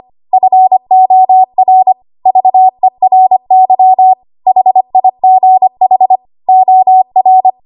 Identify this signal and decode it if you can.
Morse code, heard as a keyed tone, about 25 wpm, 750 Hz. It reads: FOR VERY HIGH OR
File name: Archive / 25 WPM / 25WPM